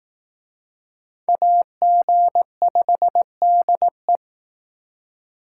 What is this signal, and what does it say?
Morse code, heard as a keyed tone, about 19 wpm, 700 Hz. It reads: AG5DE